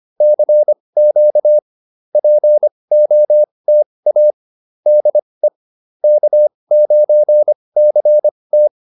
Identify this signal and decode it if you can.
Morse code, heard as a keyed tone, about 25 wpm, spaced as for 20 wpm, 600 Hz. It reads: CQ POTA DE K9CT